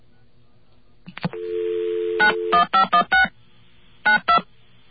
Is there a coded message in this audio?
Yes: DTMF from 2.2 to 4.4 s, heard as telephone touch tones, tones of about 110 ms. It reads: 91516 92